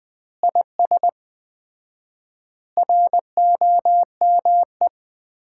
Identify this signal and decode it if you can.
Morse code, heard as a keyed tone, about 20 wpm, 700 Hz. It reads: IS ROME
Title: Common QSO Phrases - 4x Word-Spacing - 20wpm